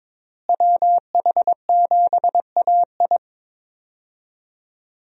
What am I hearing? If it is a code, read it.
Morse code, heard as a keyed tone, about 22 wpm, 700 Hz. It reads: WH7AI
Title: USA Call Signs - 5 Characters - Repeated 2x - 22wpm